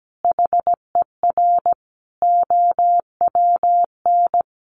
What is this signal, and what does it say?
Morse code, heard as a keyed tone, about 17 wpm, 700 Hz. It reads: HER OWN
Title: Sentences from Top 1100 Words 17wpm